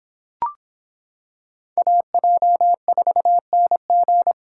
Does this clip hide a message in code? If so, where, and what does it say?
Yes: Morse code from 1.8 to 4.3 s, heard as a keyed tone, about 25 wpm, 700 Hz. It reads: AJ4NG